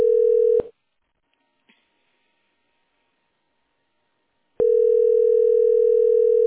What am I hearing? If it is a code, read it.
Ringback tone (North America), 440 + 480 Hz.